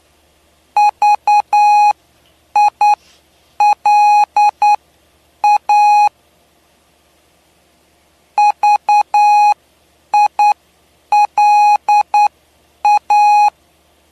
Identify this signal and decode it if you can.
Morse code, heard as a keyed tone, about 9 wpm, spaced as for 8 wpm, 830 Hz. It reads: VILA VILA
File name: cw / py2lcd / 07-PY2LCD-Li��o 7